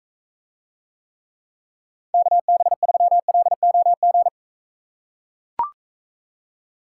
Morse code, heard as a keyed tone, about 40 wpm, 700 Hz. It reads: KB3LOG